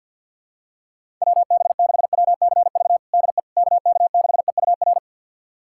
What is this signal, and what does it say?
Morse code, heard as a keyed tone, about 52 wpm, 700 Hz. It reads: WB6WCV DE KK6EUR
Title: USA Callsign DE Callsign - 6 Characters - 50wpm